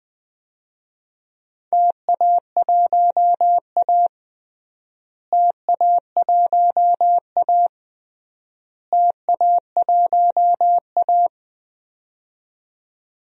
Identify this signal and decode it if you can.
Morse code, heard as a keyed tone, about 20 wpm, 700 Hz. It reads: TA1A TA1A TA1A
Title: DXCC-100 Call Signs - 4 Characters - Repeated 3x - 20wpm